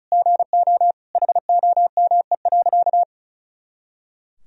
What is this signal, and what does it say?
Morse code, heard as a keyed tone, about 36 wpm, 700 Hz. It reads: GO HOME.
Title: Sentences from Top 200 Words and Abbreviations 35wpm